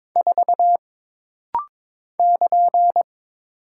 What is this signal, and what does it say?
Morse code, heard as a keyed tone, about 23 wpm, 700 Hz. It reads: 4 <KN>
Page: This rapid-fire timing is geared to push your ICR (Instant Character Recognition) skills.